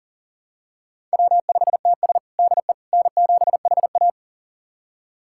Morse code, heard as a keyed tone, about 40 wpm, 700 Hz. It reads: W5TS DE N7HA
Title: USA Callsign DE Callsign - 4 Characters - 40wpm